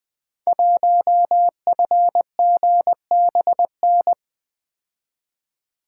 Morse code, heard as a keyed tone, about 20 wpm, 700 Hz. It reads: 1FGBN
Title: Code Groups - 5 Letter:Numbers - 20wpm